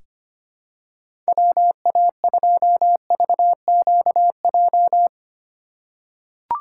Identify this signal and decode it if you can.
Morse code, heard as a keyed tone, about 25 wpm, 700 Hz. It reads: WA2VQJ
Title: USA Call Signs - 6 Characters - 4x Character Spacing - 25wpm